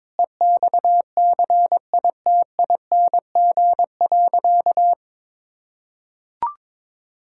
Morse code, heard as a keyed tone, about 22 wpm, 700 Hz. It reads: EXCITING.
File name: practice / 22wpm / Sentences with 4-Word Phrases - 22wpm